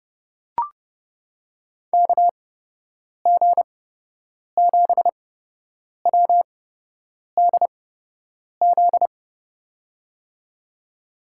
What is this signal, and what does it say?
Morse code, heard as a keyed tone, about 32 wpm, spaced as for 8 wpm, 700 Hz. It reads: KG7WDZ
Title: USA Call Signs - 6 Characters - 8x Character Spacing - 30wpm